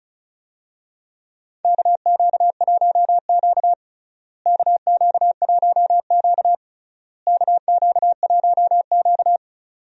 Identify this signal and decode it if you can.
Morse code, heard as a keyed tone, about 34 wpm, 700 Hz. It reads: KQ1Q KQ1Q KQ1Q
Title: USA Call Signs - 4 Characters - Repeated 3x - 35wpm